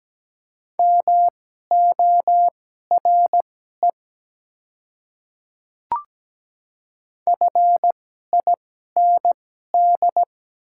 Morse code, heard as a keyed tone, about 17 wpm, spaced as for 12 wpm, 700 Hz. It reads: MORE FIND
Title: 2x Character Spacing - Top 300 Words - 17wpm